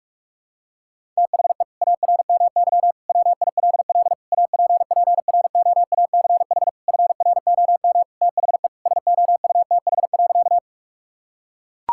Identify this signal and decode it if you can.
Morse code, heard as a keyed tone, about 41 wpm, 700 Hz. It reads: THE ARMY WILL APPROACH FROM THE SOUTH.
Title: Sentences from Top 1000 Words - Review Part 6 - 45wpm